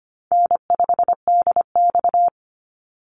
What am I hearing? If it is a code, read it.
Morse code, heard as a keyed tone, about 25 wpm, 700 Hz. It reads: N5DX
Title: Easy Call Signs Repeated 3x - 25wpm